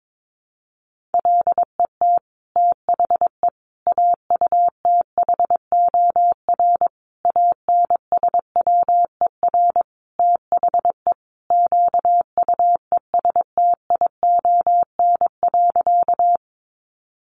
Morse code, heard as a keyed tone, about 22 wpm, 700 Hz. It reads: LET THE AUTHOR ANSWER THE QUESTION.
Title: Sentences from Top 1200 Words 22wpm